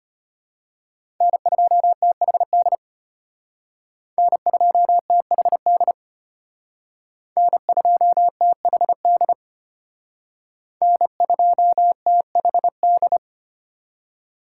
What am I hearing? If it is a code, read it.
Morse code, heard as a keyed tone, about 30 wpm, 700 Hz. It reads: N2THD N2THD N2THD N2THD
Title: Speed Racing - USA 1x3 Call Signs 25wpm